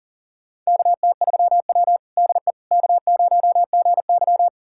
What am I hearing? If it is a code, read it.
Morse code, heard as a keyed tone, about 40 wpm, 700 Hz. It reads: KT3W DE K0GY
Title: USA Callsign DE Callsign - 4 Characters - 40wpm